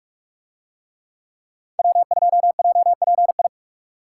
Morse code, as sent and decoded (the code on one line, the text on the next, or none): .-- ..--- .--- .--. ..
W2JPI